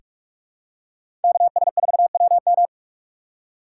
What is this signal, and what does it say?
Morse code, heard as a keyed tone, about 46 wpm, 700 Hz. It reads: KS4WM
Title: USA Call Signs - 5 Characters - 4x Character Spacing - 45wpm